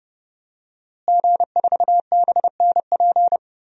Morse code, heard as a keyed tone, about 30 wpm, 700 Hz. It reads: G4BNP
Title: DXCC-100 Call Signs - 5 Characters - Repeated 3x - 30wpm